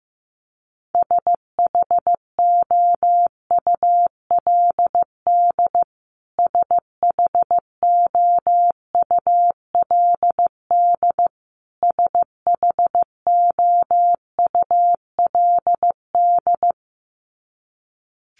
Morse code, heard as a keyed tone, about 15 wpm, 700 Hz. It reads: SHOULD SHOULD SHOULD
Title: Top 500 Words Repeated 15wpm